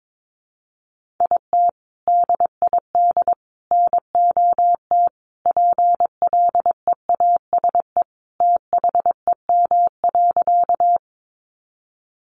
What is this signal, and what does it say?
Morse code, heard as a keyed tone, about 22 wpm, 700 Hz. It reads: IT DID NOT PLEASE THEM.
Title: Sentences from Top 1500 Words 22wpm